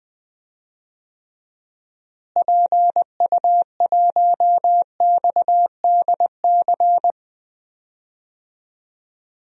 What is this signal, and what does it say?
Morse code, heard as a keyed tone, about 20 wpm, 700 Hz. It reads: PU1XDC
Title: DXCC-100 Call Signs - 6 Characters - Repeated 2x - 20wpm